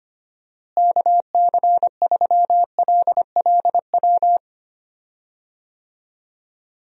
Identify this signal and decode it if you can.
Morse code, heard as a keyed tone, about 25 wpm, 700 Hz. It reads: KC3LLW
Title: USA Call Signs - 6 Characters - Repeated 3x - 25wpm